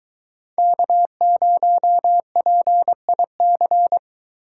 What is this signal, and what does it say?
Morse code, heard as a keyed tone, about 22 wpm, 700 Hz. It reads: K0PIC